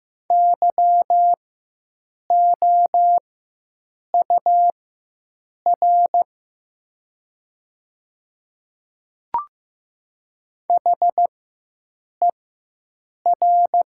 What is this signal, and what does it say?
Morse code, heard as a keyed tone, about 15 wpm, spaced as for 7 wpm, 700 Hz. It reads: YOUR HER
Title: 4x Character Spacing - Top 100 Words - 15wpm